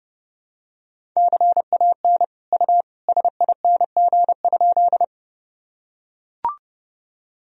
Morse code, heard as a keyed tone, about 30 wpm, 700 Hz. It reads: CAN U SING?
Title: Sentences from Top 400 Words and Abbreviations 30wpm